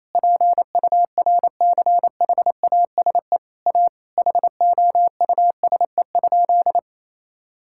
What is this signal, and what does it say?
Morse code, heard as a keyed tone, about 28 wpm, 700 Hz. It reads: PURCHASE A HOUSE?
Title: Sentences from Top 1000 Words - Review Part 6 - 28wpm